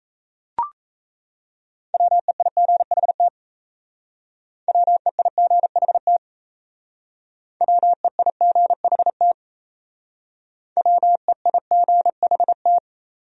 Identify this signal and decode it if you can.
Morse code, heard as a keyed tone, about 35 wpm, 690 Hz. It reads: WEIGHT WEIGHT WEIGHT WEIGHT
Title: Speed-Racing Top 500 Words 28wpm